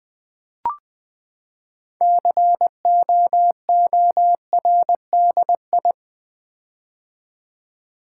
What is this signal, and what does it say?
Morse code, heard as a keyed tone, about 20 wpm, 700 Hz. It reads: COORDI
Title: Build A Word - 6 Letters At A Time - 12-Letter Words - 20wpm